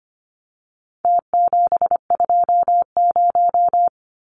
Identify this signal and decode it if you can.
Morse code, heard as a keyed tone, about 25 wpm, 700 Hz. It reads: T720
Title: Numbers - Contest Style - 4-Digit 25wpm